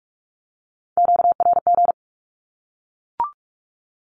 Morse code, heard as a keyed tone, about 44 wpm, 700 Hz. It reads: QRG
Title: Q-Codes 45wpm